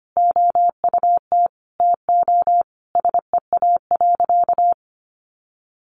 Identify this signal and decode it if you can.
Morse code, heard as a keyed tone, about 25 wpm, 700 Hz. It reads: OUT TO SEA.
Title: Sentences from Top 400 Words 25wpm